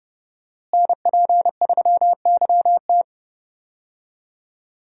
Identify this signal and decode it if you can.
Morse code, heard as a keyed tone, about 30 wpm, 700 Hz. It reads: NP3YT
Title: USA Call Signs - 5 Characters - Repeated 2x - 30wpm